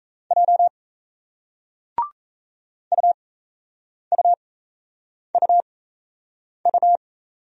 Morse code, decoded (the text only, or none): J UUUU